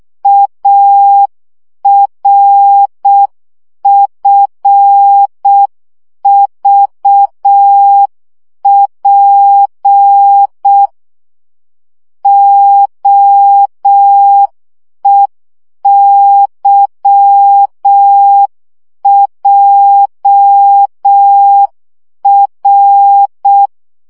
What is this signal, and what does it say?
Morse code, heard as a keyed tone, about 6 wpm, 800 Hz. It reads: ARFVP OEYJR
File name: morze vizsga 2011-2 5